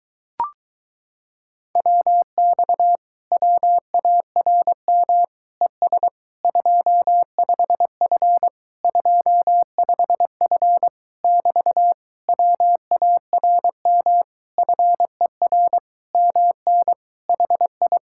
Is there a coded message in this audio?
Yes: Morse code from 1.8 to 18.0 s, heard as a keyed tone, about 24 wpm, 700 Hz. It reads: WX WARM ES 25F 25F <BT> WARM FER MN HI